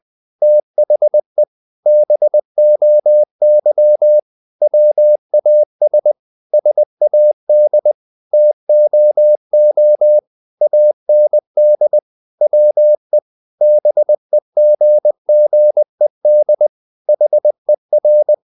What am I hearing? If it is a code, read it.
Morse code, heard as a keyed tone, about 20 wpm, 600 Hz. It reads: THE BOY WAS SAD TOO AND WE BEGGED HER
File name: the old man and the sea / the old man and the sea part 1 20wpm